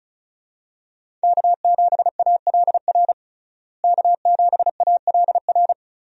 Morse code, heard as a keyed tone, about 35 wpm, 700 Hz. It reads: K7ALR K7ALR
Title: USA Call Signs - 5 Characters - Repeated 2x - 35wpm